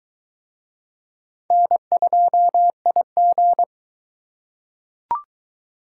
Morse code, heard as a keyed tone, about 22 wpm, 700 Hz. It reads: N2IG